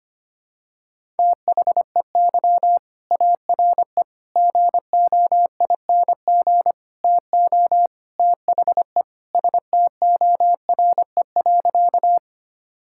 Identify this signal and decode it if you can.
Morse code, heard as a keyed tone, about 25 wpm, 700 Hz. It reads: THEY ARE GOING TO THE STORE.
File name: practice / 25wpm / Sentences with 4-Word Phrases - 25wpm